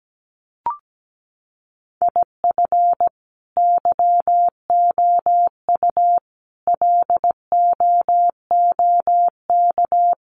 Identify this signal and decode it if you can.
Morse code, heard as a keyed tone, about 17 wpm, 700 Hz. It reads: IF YOU LOOK
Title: Sets of 5 Words 17wpm